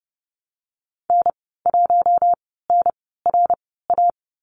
Morse code, heard as a keyed tone, about 31 wpm, spaced as for 17 wpm, 700 Hz. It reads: N1NRA
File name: Lesson 14g - Number 2 - Call Signs 30wpm